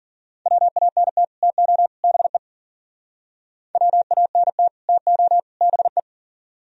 Morse code, heard as a keyed tone, about 49 wpm, 700 Hz. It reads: WANT TO BE WANT TO BE
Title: Speed-Racing - Sets of 3 Words 40wpm